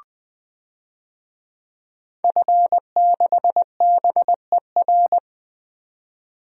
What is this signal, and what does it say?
Morse code, heard as a keyed tone, about 20 wpm, 700 Hz. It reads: F6BER
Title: DXCC-100 Call Signs - 5 Characters - 20wpm